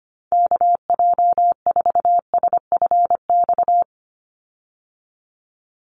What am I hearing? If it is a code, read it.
Morse code, heard as a keyed tone, about 25 wpm, 690 Hz. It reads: KJ4SFX
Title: USA Call Signs 25wpm